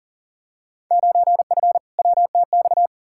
Morse code, heard as a keyed tone, about 40 wpm, 700 Hz. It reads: 9F WTX
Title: Field Day Exchanges Encore - 40wpm